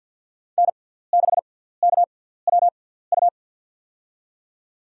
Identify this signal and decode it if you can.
Morse code, heard as a keyed tone, about 52 wpm, spaced as for 17 wpm, 700 Hz. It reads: N6KWU
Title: USA Call Signs - 5 Characters - 6x Character Spacing - 50wpm